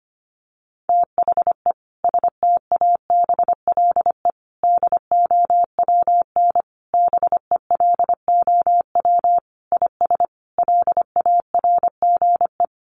Morse code, heard as a keyed tone, about 25 wpm, 700 Hz. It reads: THE STABLE DOWN BELOW IS LARGE
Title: Sentences from Top 1500 Words 25wpm